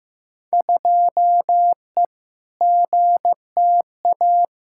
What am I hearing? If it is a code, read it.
Morse code, heard as a keyed tone, about 15 wpm, 700 Hz. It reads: 2E GTA
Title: Field Day Exchanges Encore - 15wpm